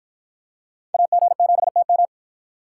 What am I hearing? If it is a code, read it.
Morse code, heard as a keyed tone, about 46 wpm, 700 Hz. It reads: AG7TM